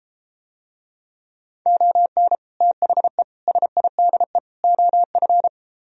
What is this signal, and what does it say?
Morse code, heard as a keyed tone, about 33 wpm, 700 Hz. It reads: ON THE SIDE OF